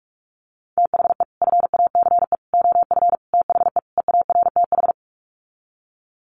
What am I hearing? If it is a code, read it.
Morse code, heard as a keyed tone, about 44 wpm, 700 Hz. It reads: THE FACE OF THE EARTH
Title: Sets of 5 Words Encore 45wpm